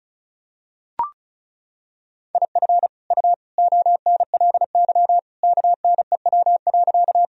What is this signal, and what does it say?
Morse code, heard as a keyed tone, about 36 wpm, 700 Hz. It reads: IF U ONLY KNEW.
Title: Sentences from Top 400 Words and Abbreviations 35wpm